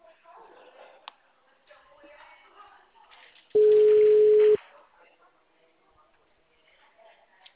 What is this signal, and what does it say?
Ringback tone (Europe, 425 Hz), with speech.